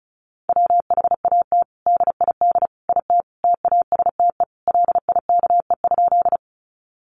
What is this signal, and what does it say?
Morse code, heard as a keyed tone, about 34 wpm, 700 Hz. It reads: WHAT DID IT TASTE LIKE?
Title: Sentences from Top 1100 Words 35wpm